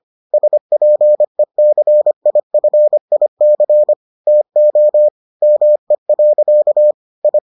Morse code, heard as a keyed tone, about 25 wpm, 600 Hz. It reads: SPECIFIC TO ME. I